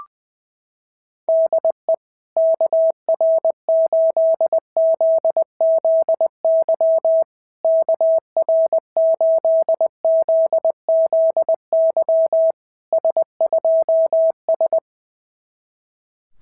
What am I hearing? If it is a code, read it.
Morse code, heard as a keyed tone, about 20 wpm, 690 Hz. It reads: DE KR8ZZY KR8ZZY S2S
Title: SOTA QSO Practice 20wpm